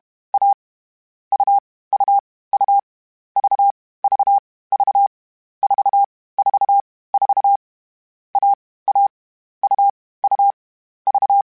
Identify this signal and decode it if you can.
Morse code, heard as a keyed tone, about 32 wpm, 800 Hz. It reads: A U U U V V V 4 4 4 A A U U V
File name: Difficult Combos / Difficult Combos-mp3s / AUV AUV4 @30